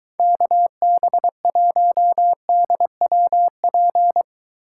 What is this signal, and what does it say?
Morse code, heard as a keyed tone, about 22 wpm, 700 Hz. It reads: KB1DWP